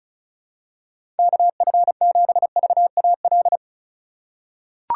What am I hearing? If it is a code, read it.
Morse code, heard as a keyed tone, about 36 wpm, 700 Hz. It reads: KF7VAL
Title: USA Call Signs - 6 Characters - 3x Character Spacing - 35wpm